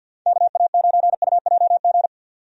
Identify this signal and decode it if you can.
Morse code, heard as a keyed tone, about 48 wpm, 700 Hz. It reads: KA9UJG
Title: USA Call Signs - 6 Characters - 50wpm